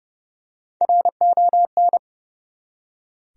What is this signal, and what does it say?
Morse code, heard as a keyed tone, about 31 wpm, 700 Hz. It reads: RON
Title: USA Common Names - Max 3 Letters - 30wpm